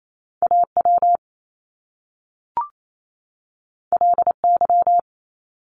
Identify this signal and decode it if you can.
Morse code, heard as a keyed tone, about 28 wpm, 700 Hz. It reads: AW LY
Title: Top Two Letter Combinations in Words 28wpm